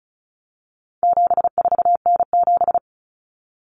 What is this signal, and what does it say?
Morse code, heard as a keyed tone, about 34 wpm, 700 Hz. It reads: 74N7
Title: Common Cut Numbers - 4-Digit 35wpm